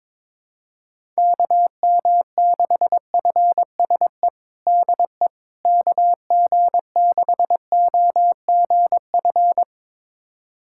Morse code, heard as a keyed tone, about 22 wpm, 700 Hz. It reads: KM6FSE DE KG6OGF
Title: USA Callsign DE Callsign - 6 Characters - 22wpm